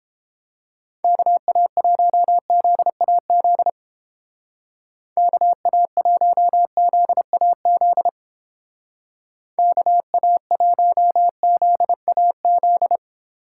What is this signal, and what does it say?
Morse code, heard as a keyed tone, about 30 wpm, 700 Hz. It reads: KA1ZAZ KA1ZAZ KA1ZAZ